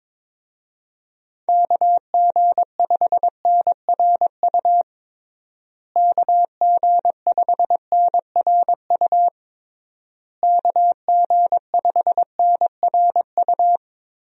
Morse code, heard as a keyed tone, about 22 wpm, 700 Hz. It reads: KG5NRU KG5NRU KG5NRU
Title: USA Call Signs - 6 Characters - Repeated 3x - 22wpm